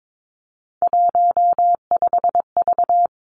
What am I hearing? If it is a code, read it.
Morse code, heard as a keyed tone, about 22 wpm, 700 Hz. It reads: 15V
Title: Three Challenging Letter-Number Combinations 22wpm